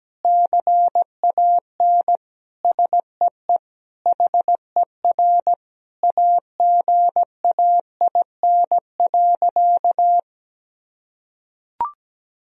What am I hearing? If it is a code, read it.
Morse code, heard as a keyed tone, about 17 wpm, 700 Hz. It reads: CAN SEE HER AGAIN.
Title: Sentences from Top 1000 Words - Review Part 3 - 17wpm